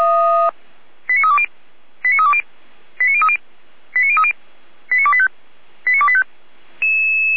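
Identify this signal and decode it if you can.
Five-tone selective call (ZVEI), heard as quick runs of five tones, tones of about 70 ms. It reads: 89320 89320 89030 89030 89285 89285
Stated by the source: ZVEI five-tone selective call received on an SDR.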